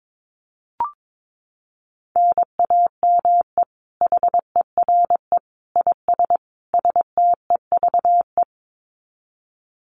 Morse code, heard as a keyed tone, about 22 wpm, 700 Hz. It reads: NAME HERE IS STEVE
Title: Common QSO Phrases 22wpm